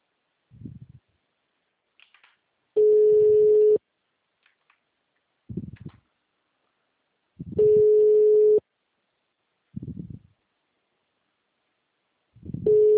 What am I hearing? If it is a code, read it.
Ringback tone (Europe, 425 Hz).